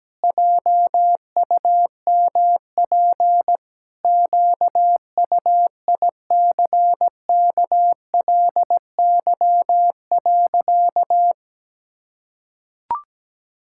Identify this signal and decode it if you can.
Morse code, heard as a keyed tone, about 17 wpm, 700 Hz. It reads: JUMP QUICKLY.
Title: Pangram Challenge v2 - 17wpm